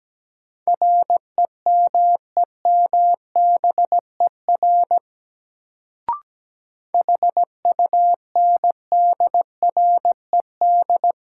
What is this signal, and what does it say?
Morse code, heard as a keyed tone, about 17 wpm, 700 Hz. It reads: REMEMBER HUNDRED
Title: Sentences from Top 400 Words - New Word Drill - 17wpm